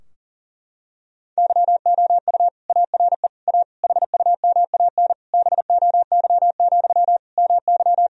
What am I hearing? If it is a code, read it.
Morse code, heard as a keyed tone, about 40 wpm, 700 Hz. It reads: YOU ARE A HUMAN BOY, MY
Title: Sentences from Top 1000 Words - Review Part 3 - 40wpm